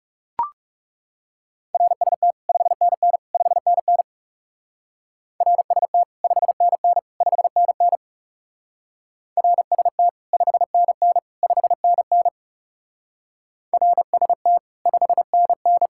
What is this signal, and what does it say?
Morse code, heard as a keyed tone, about 34 wpm, 700 Hz. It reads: RST 5NN 5NN RST 5NN 5NN RST 5NN 5NN RST 5NN
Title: Speed-Racing Common QSO Phrases - 30wpm